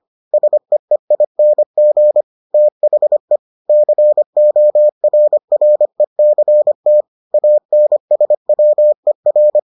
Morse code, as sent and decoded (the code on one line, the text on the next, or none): ... . .. -. --. / - .... / -.-. --- .-. .-. -.-. - / .- -. ... .-- . .-.
SEEING THE CORRECT ANSWER